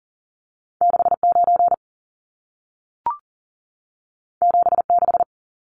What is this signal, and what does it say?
Morse code, heard as a keyed tone, about 40 wpm, 700 Hz. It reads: 69 76
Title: Numbers - 2-Digit 40wpm